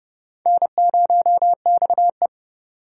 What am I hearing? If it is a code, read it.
Morse code, heard as a keyed tone, about 31 wpm, 700 Hz. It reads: N0XE